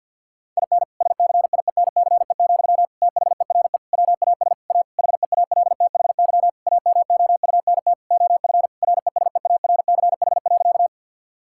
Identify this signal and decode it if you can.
Morse code, heard as a keyed tone, about 50 wpm, 700 Hz. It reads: IN SCIENCE, THERE WAS A HEALTHY AMOUNT OF RESEARCH.
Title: Sentences from Top 1000 Words - Review Part 5 - 50wpm